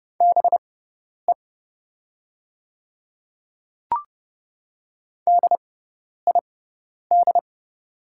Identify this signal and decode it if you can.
Morse code, heard as a keyed tone, about 30 wpm, spaced as for 10 wpm, 700 Hz. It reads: BE DID
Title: 6x Character Spacing - Top 100 Words - 30wpm